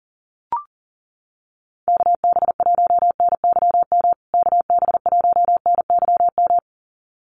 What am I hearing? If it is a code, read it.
Morse code, heard as a keyed tone, about 40 wpm, 700 Hz. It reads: KB1NYM KB1NYM
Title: USA Call Signs Repeated 2x 40wpm